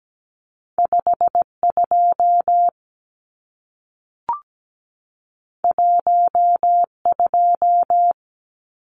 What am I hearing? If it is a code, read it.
Morse code, heard as a keyed tone, about 17 wpm, 700 Hz. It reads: 52 12
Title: Numbers - 2-Digit 17wpm